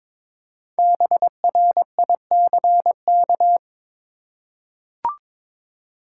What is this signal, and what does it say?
Morse code, heard as a keyed tone, about 22 wpm, 700 Hz. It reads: BRICK
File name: practice / 22wpm / 5 Letter Words - First 3 Letters Given - 22wpm